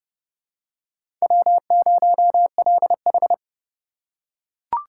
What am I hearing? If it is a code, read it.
Morse code, heard as a keyed tone, about 27 wpm, 700 Hz. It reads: W0LH